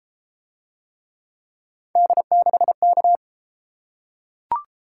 Morse code, heard as a keyed tone, about 31 wpm, 700 Hz. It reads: D6K